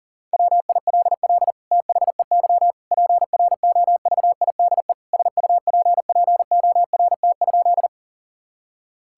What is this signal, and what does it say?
Morse code, heard as a keyed tone, about 40 wpm, 700 Hz. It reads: WILL THEY PROVIDE SUPPORT?
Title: Sentences from Top 1000 Words - Review Part 5 - 40wpm